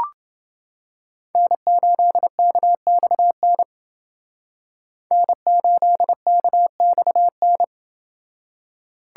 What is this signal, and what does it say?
Morse code, heard as a keyed tone, about 28 wpm, 700 Hz. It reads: N8KXN N8KXN